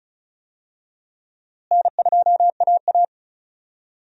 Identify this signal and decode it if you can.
Morse code, heard as a keyed tone, about 34 wpm, 700 Hz. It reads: N2AA
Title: USA Call Signs - 4 Characters - Repeated 3x - 35wpm